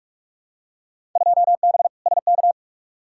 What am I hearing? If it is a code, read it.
Morse code, heard as a keyed tone, about 46 wpm, 700 Hz. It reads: 2B SK
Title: Field Day Exchanges Encore - 45wpm